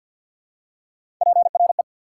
Morse code, heard as a keyed tone, about 54 wpm, 700 Hz. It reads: PRE
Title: Prefixes and Suffixes 50wpm